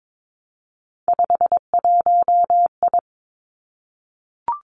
Morse code, heard as a keyed tone, about 22 wpm, 700 Hz. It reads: 51I
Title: Three Challenging Letter-Number Combinations 22wpm